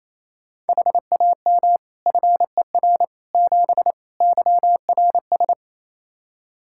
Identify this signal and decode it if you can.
Morse code, heard as a keyed tone, about 28 wpm, 700 Hz. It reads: HAM FER 7 YRS
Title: Practice QSO - Exchange 3 - More Info - 28wpm